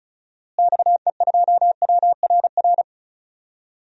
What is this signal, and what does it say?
Morse code, heard as a keyed tone, about 34 wpm, 700 Hz. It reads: XE2WRR
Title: DXCC-100 Call Signs - 6 Characters - Repeated 2x - 35wpm